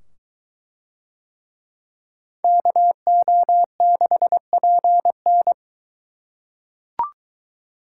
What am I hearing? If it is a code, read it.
Morse code, heard as a keyed tone, about 22 wpm, 700 Hz. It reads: KO6PN